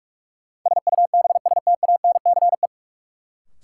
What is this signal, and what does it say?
Morse code, heard as a keyed tone, about 46 wpm, 700 Hz. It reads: SUBSTANCE
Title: Sentences from Top 1500 Words - New Word Drill Alt - 45wpm